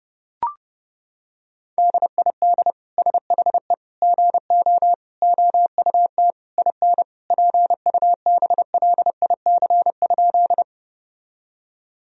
Morse code, heard as a keyed tone, about 30 wpm, 700 Hz. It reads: DID SHE GO OUT IN PUBLIC?
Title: Sentences from Top 1000 Words - Review Part 3 - 30wpm